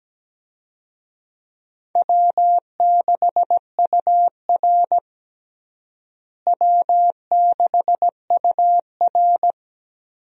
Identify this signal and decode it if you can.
Morse code, heard as a keyed tone, about 17 wpm, 700 Hz. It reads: W6UR W6UR
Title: USA Call Signs - 4 Characters - Repeated 3x - 17wpm